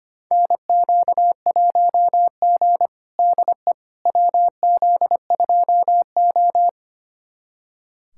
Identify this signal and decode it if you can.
Morse code, heard as a keyed tone, about 25 wpm, 700 Hz. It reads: NQ1G DE WZ2O
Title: USA Callsign DE Callsign - 4 Characters - 25wpm